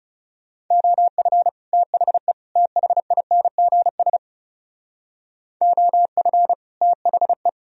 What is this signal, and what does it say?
Morse code, heard as a keyed tone, about 35 wpm, 700 Hz. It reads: OF THE THINGS OF THE
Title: Speed-Racing - Sets of 3 Words 30wpm